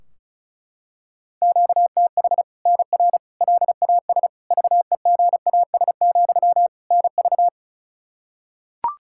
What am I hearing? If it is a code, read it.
Morse code, heard as a keyed tone, about 36 wpm, 700 Hz. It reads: QTH NR LAS VEGAS, NV
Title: Practice QSO - Exchange 1 - Beginning - 35wpm